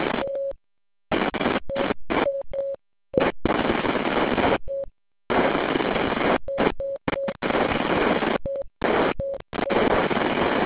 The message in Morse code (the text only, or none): none